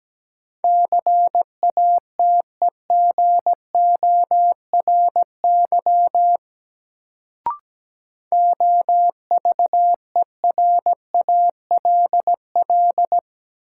Morse code, heard as a keyed tone, about 17 wpm, 700 Hz. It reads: CATEGORY OVERALL
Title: Sentences from Top 1000 Words - Review Part 7 - New Word Drill - 17wpm